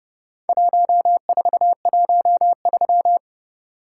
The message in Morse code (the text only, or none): 1413